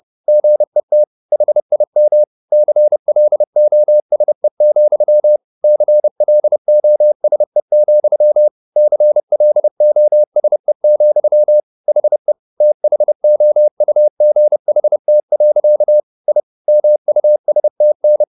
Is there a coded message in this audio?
Yes: Morse code from 0.3 to 18.2 s, heard as a keyed tone, about 30 wpm, 600 Hz. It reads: GET HIM CLOSE, CLOSE, CLOSE, HE THOUGHT. I MUSTN